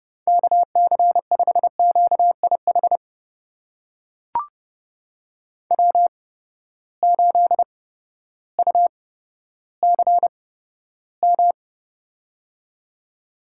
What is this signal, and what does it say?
Morse code, heard as a keyed tone, about 29 wpm, spaced as for 14 wpm, 700 Hz. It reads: KC5QIH W8UCM
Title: USA Call Signs - 8x Character Spacing - 30wpm